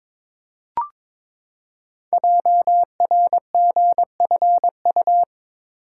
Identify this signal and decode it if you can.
Morse code, heard as a keyed tone, about 22 wpm, 690 Hz. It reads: JRGFU
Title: Code Groups - 5 Letters - 22wpm